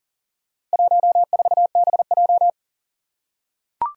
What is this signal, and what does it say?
Morse code, heard as a keyed tone, about 40 wpm, 700 Hz. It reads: 14BJ
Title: Four Challenging Letter-Number Combinations 40wpm